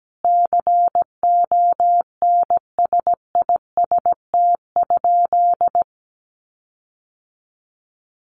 Morse code, heard as a keyed tone, about 17 wpm, 700 Hz. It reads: CONSIST?
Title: Sentences from Top 1400 Words 17wpm